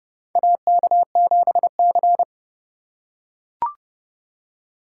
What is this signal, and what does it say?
Morse code, heard as a keyed tone, about 30 wpm, 700 Hz. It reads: AK7C
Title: USA Call Signs - 4 Characters - 4x Character Spacing - 30wpm